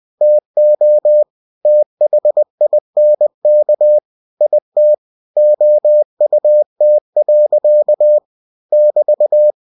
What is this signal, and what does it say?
Morse code, heard as a keyed tone, about 20 wpm, 600 Hz. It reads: TO THINK IT OUT. <BT>
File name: cw-books / winnie-the-pooh / winnie-20wpm-page-0034